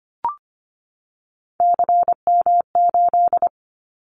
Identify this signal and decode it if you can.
Morse code, heard as a keyed tone, about 25 wpm, 700 Hz. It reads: CM8
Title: Callsign Prefixes for Top 100 DXCC - 25wpm